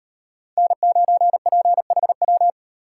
Morse code, heard as a keyed tone, about 38 wpm, 700 Hz. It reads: N9PHW